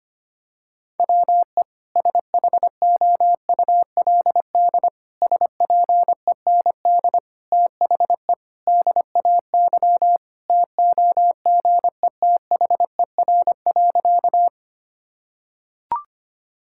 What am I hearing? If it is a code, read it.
Morse code, heard as a keyed tone, about 25 wpm, 700 Hz. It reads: WE SHOULD SPEND THE DAY TOGETHER.
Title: Sentences from Top 1000 Words - Review Part 5 - 25wpm